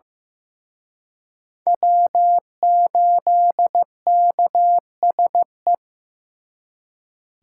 Morse code, heard as a keyed tone, about 18 wpm, 700 Hz. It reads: W8KSE
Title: Speed Racing - USA 1x3 Call Signs 15wpm